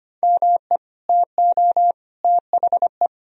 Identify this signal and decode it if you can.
Morse code, heard as a keyed tone, about 25 wpm, 700 Hz. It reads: ME TO THE
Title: Sentences from Top 1000 Words - Review Part 2 - 25wpm